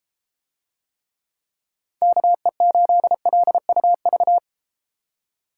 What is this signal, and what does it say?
Morse code, heard as a keyed tone, about 30 wpm, 700 Hz. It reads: KE8LUV